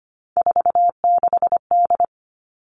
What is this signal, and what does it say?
Morse code, heard as a keyed tone, about 25 wpm, 700 Hz. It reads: 46D
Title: Three Challenging Letter-Number Combinations 25wpm